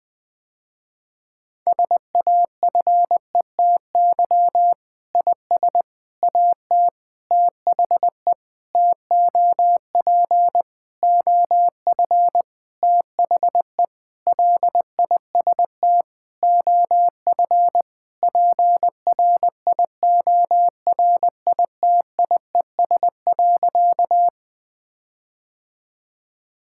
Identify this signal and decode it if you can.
Morse code, heard as a keyed tone, about 20 wpm, 700 Hz. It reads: SAFETY IS AT THE TOP OF THE LIST OF PRIORITIES.
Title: Sentences with 7-Word Phrases - Spoken Beginning - 20wpm